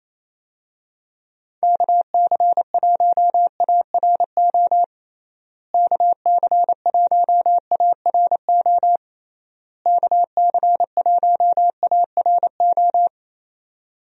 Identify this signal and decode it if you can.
Morse code, heard as a keyed tone, about 28 wpm, 700 Hz. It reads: KC1ARO KC1ARO KC1ARO
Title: USA Call Signs - 6 Characters - Repeated 3x - 28wpm